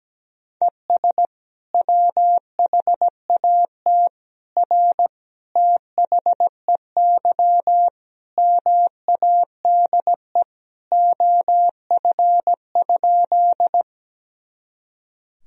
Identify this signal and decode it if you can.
Morse code, heard as a keyed tone, about 17 wpm, 700 Hz. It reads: ES WHAT R THEY MADE OF?
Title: Sentences from Top 200 Words and Abbreviations 17wpm